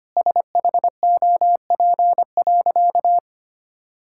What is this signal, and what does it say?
Morse code, heard as a keyed tone, about 25 wpm, 700 Hz. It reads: SHOP.
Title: Sentences from Top 1600 Words 25wpm